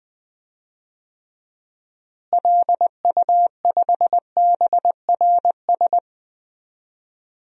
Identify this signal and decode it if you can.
Morse code, heard as a keyed tone, about 20 wpm, 700 Hz. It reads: LU5BRS
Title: DXCC-100 Call Signs - 6 Characters - Repeated 3x - 20wpm